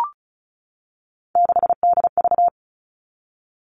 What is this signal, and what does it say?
Morse code, heard as a keyed tone, about 34 wpm, 700 Hz. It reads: 6DV
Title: All Cut Numbers - 3-Digit 35wpm